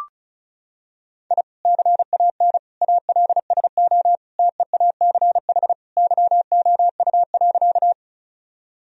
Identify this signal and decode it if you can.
Morse code, heard as a keyed tone, about 34 wpm, 700 Hz. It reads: I CAN ALSO TEACH YOU.
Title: Sentences from Top 1000 Words - Review Part 2 - 35wpm